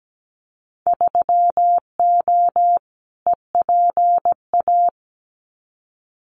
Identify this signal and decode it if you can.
Morse code, heard as a keyed tone, about 17 wpm, 700 Hz. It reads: 3O EPA
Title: Winter Field Day Exchanges Encore 17wpm